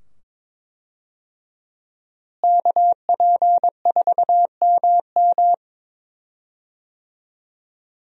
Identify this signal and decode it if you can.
Morse code, heard as a keyed tone, about 22 wpm, 700 Hz. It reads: KP4MM
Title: USA Call Signs - 5 Characters - Repeated 2x - 22wpm